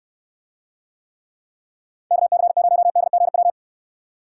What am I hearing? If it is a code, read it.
Morse code, heard as a keyed tone, about 66 wpm, 700 Hz. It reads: KC0DGW